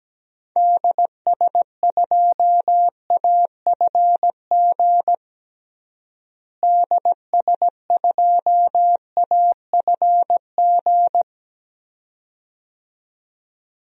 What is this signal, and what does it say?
Morse code, heard as a keyed tone, about 17 wpm, 700 Hz. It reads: DS2AFG DS2AFG
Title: DXCC-100 Call Signs - 6 Characters - Repeated 2x - 17wpm